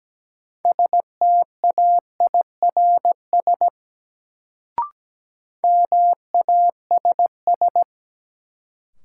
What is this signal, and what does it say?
Morse code, heard as a keyed tone, about 17 wpm, 700 Hz. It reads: STAIRS MASS
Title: Sets of 5 Words Encore - New Word Drill - 17wpm